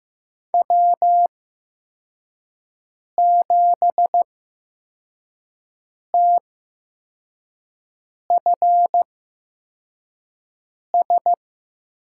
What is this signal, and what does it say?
Morse code, heard as a keyed tone, about 15 wpm, spaced as for 4 wpm, 700 Hz. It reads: W7TFS
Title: USA Call Signs - 5 Characters - 8x Character Spacing - 15wpm